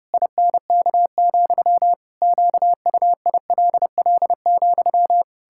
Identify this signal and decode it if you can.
Morse code, heard as a keyed tone, about 30 wpm, 700 Hz. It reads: INK, QUILL,
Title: Pangram Challenge v2 - 30wpm